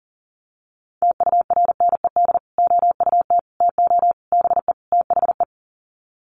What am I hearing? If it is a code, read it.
Morse code, heard as a keyed tone, about 37 wpm, 700 Hz. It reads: TURNED OUT TO BE THE
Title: Sets of 5 Words Encore 40wpm